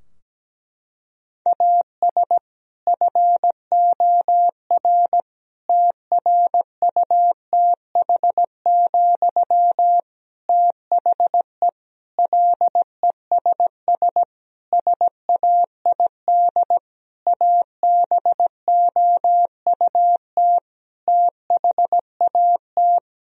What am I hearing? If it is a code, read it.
Morse code, heard as a keyed tone, about 17 wpm, 700 Hz. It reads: AS FOR TRUTH, THE LESS SAID ABOUT THAT
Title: Sentences from Top 1000 Words - Review Part 2 - 17wpm